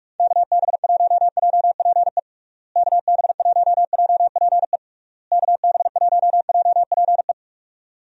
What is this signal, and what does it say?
Morse code, heard as a keyed tone, about 43 wpm, 700 Hz. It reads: KB1JPE KB1JPE KB1JPE
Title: USA Call Signs - 6 Characters - Repeated 3x - 45wpm